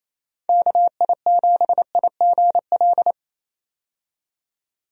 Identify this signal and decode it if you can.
Morse code, heard as a keyed tone, about 28 wpm, 700 Hz. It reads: KI7IGL
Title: USA Call Signs - 6 Characters - 28wpm